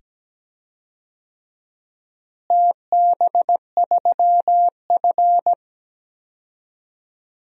Morse code, heard as a keyed tone, about 17 wpm, 700 Hz. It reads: TB3F
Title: DXCC-100 Call Signs - 4 Characters - Repeated 2x - 17wpm